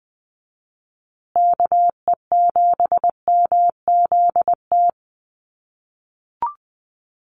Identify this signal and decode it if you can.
Morse code, heard as a keyed tone, about 20 wpm, 700 Hz. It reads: KE7MZT
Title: USA Call Signs Repeated 2x 20wpm